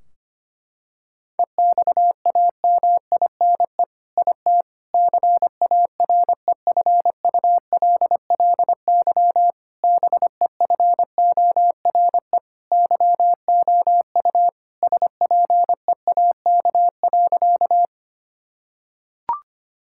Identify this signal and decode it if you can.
Morse code, heard as a keyed tone, about 25 wpm, 700 Hz. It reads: EXAMINE IT CAREFULLY BEFORE YOU SPEAK.
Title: Sentences from Top 1600 Words 25wpm